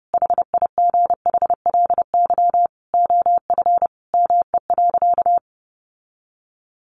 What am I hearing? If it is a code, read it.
Morse code, heard as a keyed tone, about 28 wpm, 700 Hz. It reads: HIGHLY OF ME.
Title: Sentences from Top 1100 Words 30wpm